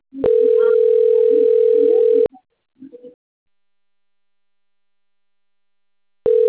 Ringback tone (North America), 440 + 480 Hz.
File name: rg-1600-unknown- - - .62589